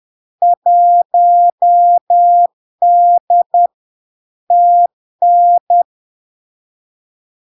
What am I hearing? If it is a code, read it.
Morse code, heard as a keyed tone, about 10 wpm, 700 Hz. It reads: 1D TN
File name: LARK-FD-2022-Log-in-CW-10wpm-700-Hz